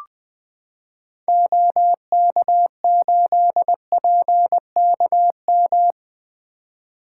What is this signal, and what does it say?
Morse code, heard as a keyed tone, about 20 wpm, 700 Hz. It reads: OK8PKM
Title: 2023 SOTA Activator Call Signs - 20wpm